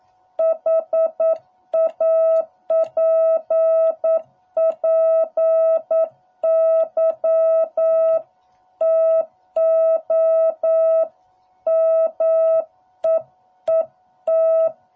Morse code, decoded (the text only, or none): HAPPY TO MEET